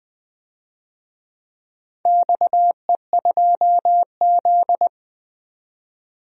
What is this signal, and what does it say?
Morse code, heard as a keyed tone, about 20 wpm, 700 Hz. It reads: XE2Z
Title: DXCC-100 Call Signs - 4 Characters - 20wpm